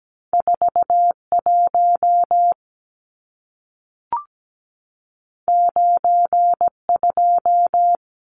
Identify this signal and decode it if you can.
Morse code, heard as a keyed tone, about 17 wpm, 700 Hz. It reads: 41 92
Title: Numbers - 2-Digit 17wpm